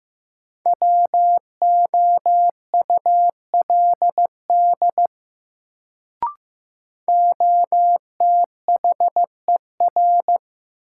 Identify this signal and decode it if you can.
Morse code, heard as a keyed tone, about 15 wpm, 700 Hz. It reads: WOULD OTHER
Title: Sentences from Top 100 Words - New Word Drill - 15wpm